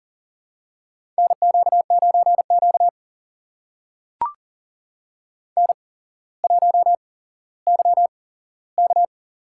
Morse code, heard as a keyed tone, about 39 wpm, spaced as for 17 wpm, 700 Hz. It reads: NQ9Q N1YK
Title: USA Call Signs - 4 Characters - 8x Character Spacing - 40wpm